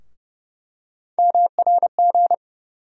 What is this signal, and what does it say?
Morse code, heard as a keyed tone, about 30 wpm, 700 Hz. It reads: MRG
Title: USA Call Sign Postfixes - 3 Letters 30wpm